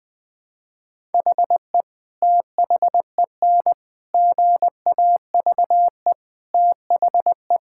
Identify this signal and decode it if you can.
Morse code, heard as a keyed tone, about 20 wpm, 700 Hz. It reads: HE THEN GAVE THE
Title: Sentences from Top 1000 Words - Review Part 6 - 20wpm